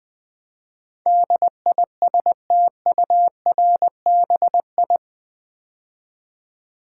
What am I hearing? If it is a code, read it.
Morse code, heard as a keyed tone, about 20 wpm, 700 Hz. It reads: DISTURBI
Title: Build A Word - 4 Letters At A Time - 10-Letter Words - 20wpm